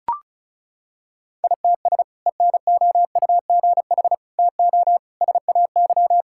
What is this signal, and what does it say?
Morse code, heard as a keyed tone, about 36 wpm, 700 Hz. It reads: ITS ENOUGH TO SAY
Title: Sentences from Top 1000 Words - Review Part 6 - 35wpm